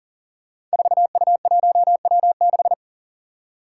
Morse code, heard as a keyed tone, about 37 wpm, 700 Hz. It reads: 4U1W6
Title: Five Challenging Letter-Number Combinations 40wpm